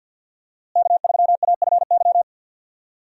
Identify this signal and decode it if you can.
Morse code, heard as a keyed tone, about 50 wpm, 700 Hz. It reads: K3AFY
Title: USA Call Signs - 5 Characters - Repeated 2x - 50wpm